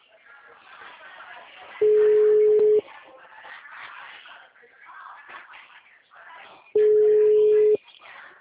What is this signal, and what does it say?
Ringback tone (Europe, 425 Hz), with speech.